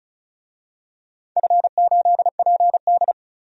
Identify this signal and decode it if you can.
Morse code, heard as a keyed tone, about 36 wpm, 700 Hz. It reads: F8PD
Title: DXCC-100 Call Signs - 4 Characters - 35wpm